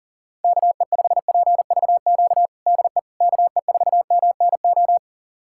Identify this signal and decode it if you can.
Morse code, heard as a keyed tone, about 40 wpm, 700 Hz. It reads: KE5PVQ DE KE4MNO
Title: USA Callsign DE Callsign 40wpm